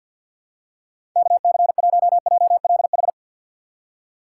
Morse code, heard as a keyed tone, about 46 wpm, 700 Hz. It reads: KC1JLH
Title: USA Call Signs - 6 Characters - 3x Character Spacing - 50wpm